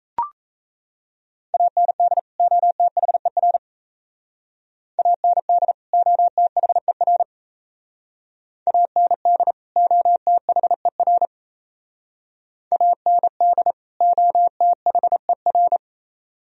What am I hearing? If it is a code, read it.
Morse code, heard as a keyed tone, about 32 wpm, 700 Hz. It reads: AND OTHER AND OTHER AND OTHER AND OTHER
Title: Speed-Racing - Sets of 2 Words 28wpm